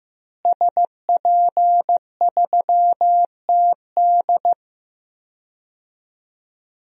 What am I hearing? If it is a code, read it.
Morse code, heard as a keyed tone, about 15 wpm, 700 Hz. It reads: SP3TD
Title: DXCC-100 Call Signs - 5 Characters - Repeated 3x - 15wpm